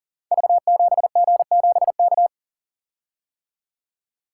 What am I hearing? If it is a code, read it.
Morse code, heard as a keyed tone, about 42 wpm, 700 Hz. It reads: V7G7K
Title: Code Groups - 5 Letter:Numbers - 40wpm